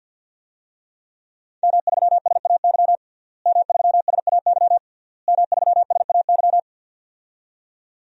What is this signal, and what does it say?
Morse code, heard as a keyed tone, about 52 wpm, 700 Hz. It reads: M3SAY M3SAY M3SAY
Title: DXCC-100 Call Signs - 5 Characters - Repeated 3x - 50wpm